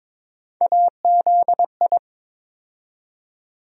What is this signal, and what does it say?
Morse code, heard as a keyed tone, about 22 wpm, 700 Hz. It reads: AZI
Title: USA Call Sign Postfixes - 3 Letters 22wpm